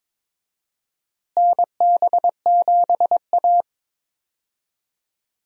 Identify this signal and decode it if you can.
Morse code, heard as a keyed tone, about 22 wpm, 700 Hz. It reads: NB7A
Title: USA Call Signs - 4 Characters - Repeated 2x - 22wpm